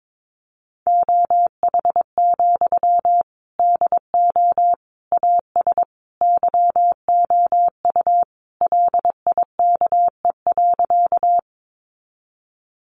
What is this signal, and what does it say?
Morse code, heard as a keyed tone, about 22 wpm, 700 Hz. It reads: OH, DO AS YOU LIKE.
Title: Sentences from Top 500 Words 22wpm